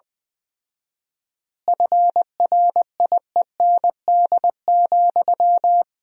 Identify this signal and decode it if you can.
Morse code, heard as a keyed tone, about 20 wpm, 700 Hz. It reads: FRIEND,
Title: 4x Word-Spacing - Sentences from Top 500 Words - 20wpm